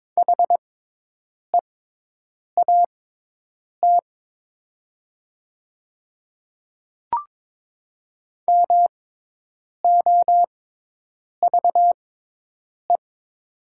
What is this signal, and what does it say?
Morse code, heard as a keyed tone, about 22 wpm, spaced as for 8 wpm, 700 Hz. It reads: HEAT MOVE
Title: 6x Character Spacing - Top 500 Words - 22wpm